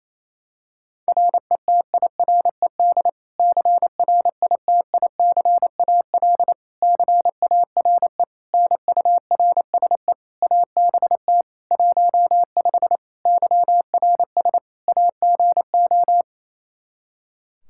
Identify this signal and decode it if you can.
Morse code, heard as a keyed tone, about 28 wpm, 700 Hz. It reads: RETIRED CRITICAL CARE NURSE ABT 15 YRS AGO
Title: Practice QSO - Exchange 3 - More Info - 28wpm